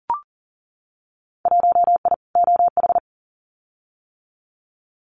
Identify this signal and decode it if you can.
Morse code, heard as a keyed tone, about 41 wpm, 700 Hz. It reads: 1I OH
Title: Winter Field Day Exchanges 40wpm